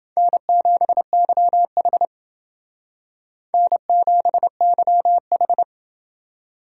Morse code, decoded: N7YH N7YH

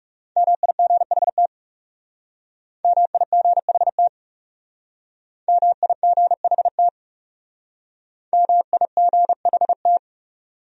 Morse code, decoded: MIGHT MIGHT MIGHT MIGHT

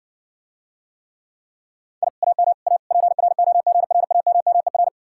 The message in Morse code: I AM A PROGRAMMER

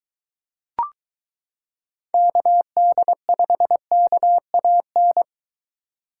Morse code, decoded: KD5KAN

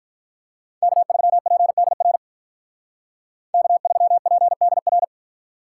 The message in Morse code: K3PDR K3PDR

